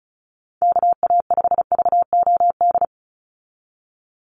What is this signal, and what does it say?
Morse code, heard as a keyed tone, about 34 wpm, 700 Hz. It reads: KA5VOD
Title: USA Call Signs 35wpm